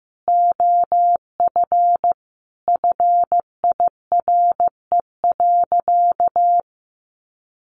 Morse code, heard as a keyed tone, about 15 wpm, 700 Hz. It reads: OF FIRE.
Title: Easy Phrases 15wpm